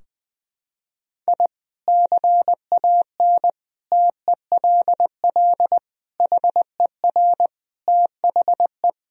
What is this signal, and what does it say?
Morse code, heard as a keyed tone, about 20 wpm, 700 Hz. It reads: I CAN TELL HER THE
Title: Sentences from Top 1000 Words - Review Part 3 - 20wpm